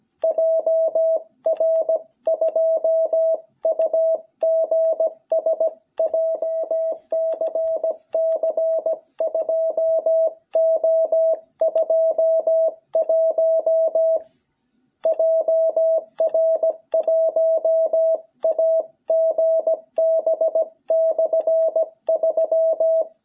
Morse code, heard as a keyed tone, about 17 wpm, 660 Hz. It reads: JR2UGS JCC 2O21 JR1AGB/3
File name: jh8cbh / 2021cw